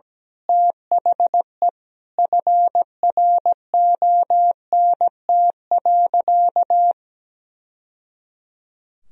Morse code, heard as a keyed tone, about 17 wpm, 700 Hz. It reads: THE FRONT.
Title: Sentences from Top 1000 Words - Review Part 2 - 17wpm